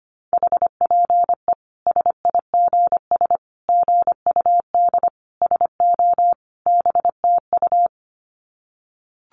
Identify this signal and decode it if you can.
Morse code, heard as a keyed tone, about 25 wpm, 700 Hz. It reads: HPE SIGS GUD SO BTU
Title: Common QSO Phrases 25wpm